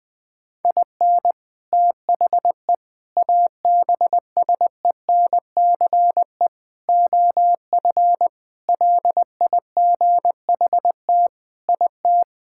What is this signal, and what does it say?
Morse code, heard as a keyed tone, about 20 wpm, 700 Hz. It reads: IN THE ABSENCE OF LIGHT IT
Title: Sentences with 4-Word Phrases - 20wpm